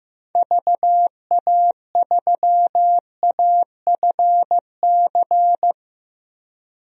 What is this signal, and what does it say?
Morse code, heard as a keyed tone, about 15 wpm, 700 Hz. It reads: VA3AFC
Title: DXCC-100 Call Signs - 6 Characters - 15wpm